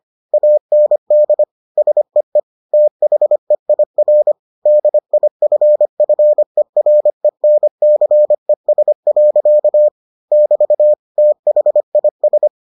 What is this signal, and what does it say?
Morse code, heard as a keyed tone, about 25 wpm, 600 Hz. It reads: AND SEE THEIR DIFFERENCES. <BT> THIS